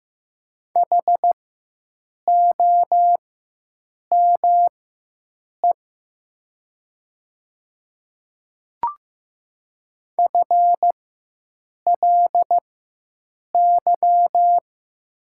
Morse code, heard as a keyed tone, about 15 wpm, spaced as for 7 wpm, 700 Hz. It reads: HOME FLY
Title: 4x Character Spacing - Top 500 Words - 15wpm